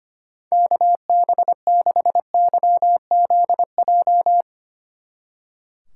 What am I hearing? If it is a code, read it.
Morse code, heard as a keyed tone, about 32 wpm, 700 Hz. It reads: KB6YZJ